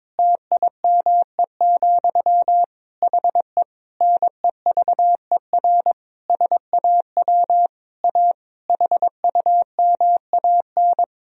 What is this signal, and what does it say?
Morse code, heard as a keyed tone, about 22 wpm, 700 Hz. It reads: TIME, HE NEVER SAW A HUMAN
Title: Sentences from Top 1000 Words - Review Part 3 - 22wpm